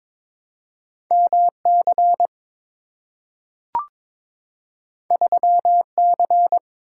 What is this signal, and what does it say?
Morse code, heard as a keyed tone, about 22 wpm, 700 Hz. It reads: MC 3C
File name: practice / 22wpm / Two Letter-Number Combinations - 22wpm